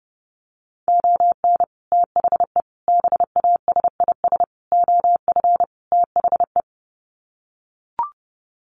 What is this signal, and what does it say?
Morse code, heard as a keyed tone, about 28 wpm, 700 Hz. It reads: ON THE BASIS OF THE
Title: Sets of 5 Words 30wpm